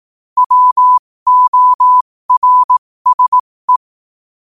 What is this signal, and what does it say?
Morse code, heard as a keyed tone, about 18 wpm, spaced as for 15 wpm, 1000 Hz. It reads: WORSE
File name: qotdfiles / 15wpm